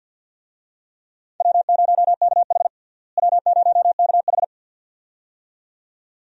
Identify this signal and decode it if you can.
Morse code, heard as a keyed tone, about 47 wpm, 700 Hz. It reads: W0KH W0KH
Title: USA Call Signs - 4 Characters - Repeated 2x - 50wpm